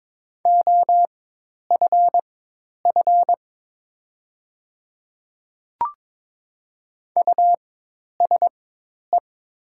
Morse code, heard as a keyed tone, about 22 wpm, spaced as for 10 wpm, 700 Hz. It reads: OFF USE
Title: 4x Character Spacing - Top 200 Words - 22wpm